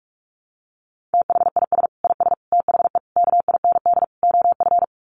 Morse code, heard as a keyed tone, about 44 wpm, 700 Hz. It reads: THIS IS THE KIND OF
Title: Sets of 5 Words Encore 45wpm